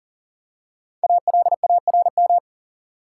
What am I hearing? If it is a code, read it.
Morse code, heard as a keyed tone, about 38 wpm, 700 Hz. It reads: ALARM
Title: Five Letter Words - 40wpm